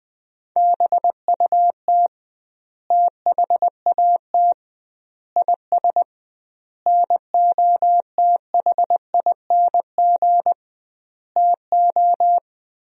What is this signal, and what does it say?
Morse code, heard as a keyed tone, about 19 wpm, 700 Hz. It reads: BUT THAT IS NOTHING TO
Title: 2x Word-Spacing - Sentences from Top 500 Words - 20wpm